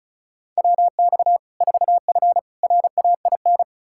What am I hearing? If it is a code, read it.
Morse code, heard as a keyed tone, about 36 wpm, 700 Hz. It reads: WX 4F RAIN
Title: Practice QSO - Exchange 2 - Info - 35wpm